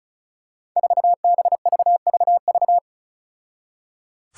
Morse code, heard as a keyed tone, about 34 wpm, 700 Hz. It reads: 4BVVV
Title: Five Challenging Letter-Number Combinations 35wpm